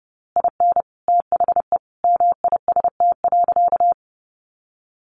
Morse code, heard as a keyed tone, about 30 wpm, 700 Hz. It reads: IN THE MIST.
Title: A Sound Of Thunder 30wpm